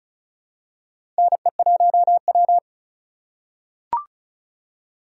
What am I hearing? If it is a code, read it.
Morse code, heard as a keyed tone, about 36 wpm, 700 Hz. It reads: NE1W
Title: USA Call Signs - 4 Characters - 2x Character Spacing - 35wpm